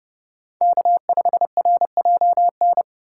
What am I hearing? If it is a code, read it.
Morse code, heard as a keyed tone, about 30 wpm, 700 Hz. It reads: K5RJN
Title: USA Call Signs - 5 Characters - 4x Character Spacing - 30wpm